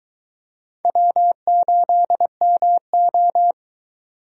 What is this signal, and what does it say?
Morse code, heard as a keyed tone, about 23 wpm, 700 Hz. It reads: W8MO